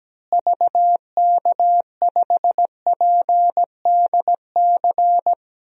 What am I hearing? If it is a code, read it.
Morse code, heard as a keyed tone, about 17 wpm, 700 Hz. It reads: VK5PDC
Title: DXCC-100 Call Signs - 6 Characters - Repeated 2x - 17wpm